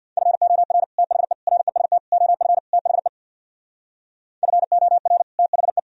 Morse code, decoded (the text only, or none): FOR THE REST OF THE FOR THE